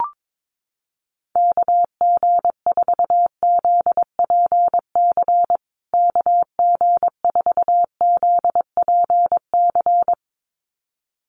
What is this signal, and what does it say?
Morse code, heard as a keyed tone, about 22 wpm, 700 Hz. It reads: KG4ZPC KG4ZPC
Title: USA Call Signs Repeated 2x 22wpm